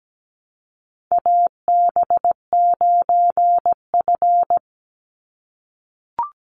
Morse code, heard as a keyed tone, about 17 wpm, 700 Hz. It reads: AB9F
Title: USA Call Signs 17wpm